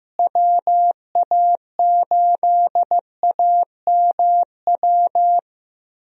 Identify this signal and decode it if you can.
Morse code, heard as a keyed tone, about 15 wpm, 700 Hz. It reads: WA8AMW
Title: USA Call Signs - 6 Characters - Repeated 3x - 15wpm